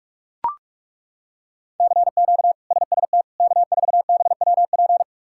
Morse code, heard as a keyed tone, about 42 wpm, 700 Hz. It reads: CQ SST K4BWP
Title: K1USN SST QSO Practice - 45wpm